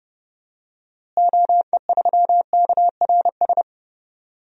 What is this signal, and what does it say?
Morse code, heard as a keyed tone, about 32 wpm, 700 Hz. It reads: OE3KRS
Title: DXCC-100 Call Signs - 6 Characters - Repeated 2x - 30wpm